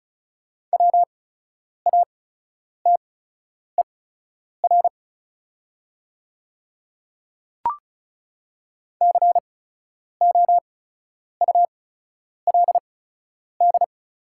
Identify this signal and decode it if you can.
Morse code, heard as a keyed tone, about 36 wpm, spaced as for 10 wpm, 690 Hz. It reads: WATER COULD
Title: 8x Character Spacing - Top 100 Words - 35wpm